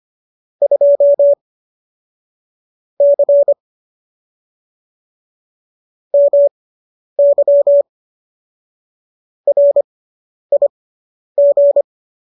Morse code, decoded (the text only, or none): 2 C MY RIG